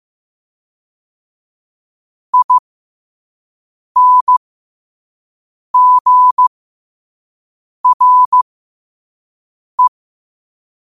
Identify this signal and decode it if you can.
Morse code, heard as a keyed tone, about 15 wpm, spaced as for 5 wpm, 1000 Hz. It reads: INGRE
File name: qotdfiles / 5wpm